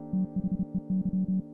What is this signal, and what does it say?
Morse code, heard as a keyed tone, about 32 wpm, 190 Hz. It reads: THEY
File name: Cooper-Moore / A Retrospective 1990-2010 / Cooper-Moore - 05 - e If Only piano and morse code